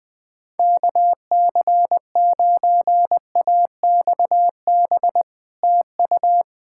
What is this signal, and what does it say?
Morse code, heard as a keyed tone, about 20 wpm, 700 Hz. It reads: KC9AXB TU
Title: Speed-Racing Common QSO Phrases Encore - 20wpm